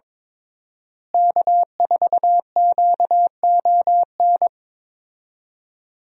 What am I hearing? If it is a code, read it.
Morse code, heard as a keyed tone, about 22 wpm, 700 Hz. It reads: K4QON
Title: USA Call Signs - 5 Characters - Repeated 2x - 22wpm